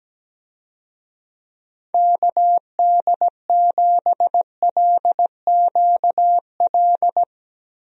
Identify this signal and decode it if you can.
Morse code, heard as a keyed tone, about 17 wpm, 700 Hz. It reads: KD7LQL
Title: USA Call Signs - 6 Characters - Repeated 2x - 17wpm